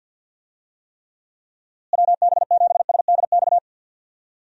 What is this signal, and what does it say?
Morse code, heard as a keyed tone, about 48 wpm, 700 Hz. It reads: WB7SDX
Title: USA Call Signs - 6 Characters - Repeated 3x - 50wpm